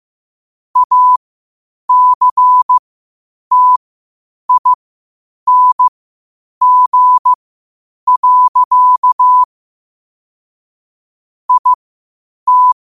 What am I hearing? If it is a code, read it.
Morse code, heard as a keyed tone, about 15 wpm, spaced as for 8 wpm, 1000 Hz. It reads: ACTING. IT
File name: qotdfiles / 8wpm